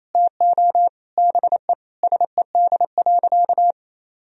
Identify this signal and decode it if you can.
Morse code, heard as a keyed tone, about 28 wpm, 700 Hz. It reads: TO BE SED.
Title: Sentences from Top 500 Words and Abbreviations 28wpm